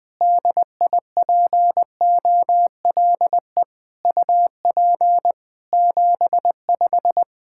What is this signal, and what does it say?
Morse code, heard as a keyed tone, about 21 wpm, 700 Hz. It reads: DIPOLE UP 75
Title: Practice QSO - Exchange 2 - Info - 20wpm